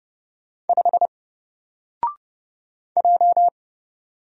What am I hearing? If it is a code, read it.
Morse code, heard as a keyed tone, about 31 wpm, 700 Hz. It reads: 5 J
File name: Single-Letter-Number-30wpm